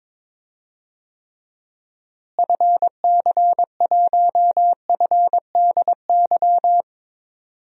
Morse code, heard as a keyed tone, about 22 wpm, 700 Hz. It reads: FC1FDY
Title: DXCC-100 Call Signs - 6 Characters - Repeated 3x - 22wpm